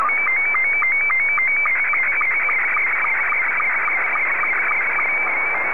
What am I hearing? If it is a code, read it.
SSTV (slow-scan television), Scottie 2.